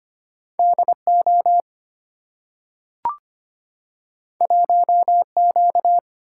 Morse code, heard as a keyed tone, about 25 wpm, 700 Hz. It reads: DO 1Q
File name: Two Letter-Number Combinations - 25wpm